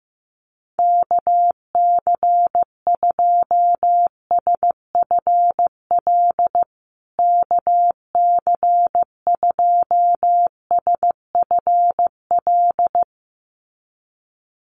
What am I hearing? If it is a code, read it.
Morse code, heard as a keyed tone, about 15 wpm, 700 Hz. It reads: KC2SFL KC2SFL
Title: USA Call Signs Repeated 2x 15wpm